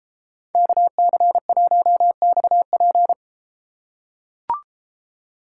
Morse code, heard as a keyed tone, about 32 wpm, 700 Hz. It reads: KC1XP